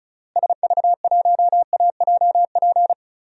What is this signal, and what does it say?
Morse code, heard as a keyed tone, about 34 wpm, 700 Hz. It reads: SV1AJP
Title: DXCC-100 Call Signs - 6 Characters - Repeated 3x - 35wpm